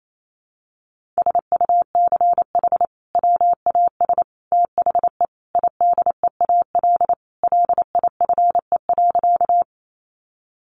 Morse code, heard as a keyed tone, about 28 wpm, 700 Hz. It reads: SUCH WAS THE IDEAL LIFE.
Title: Sentences from Top 1400 Words 28wpm